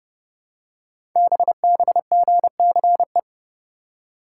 Morse code, heard as a keyed tone, about 29 wpm, 700 Hz. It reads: BBGCE
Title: Code Groups - 5 Letter:Numbers - 30wpm